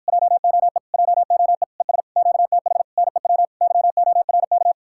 Morse code, heard as a keyed tone, about 52 wpm, 700 Hz. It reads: JOE JOE ES QTH NEW YORK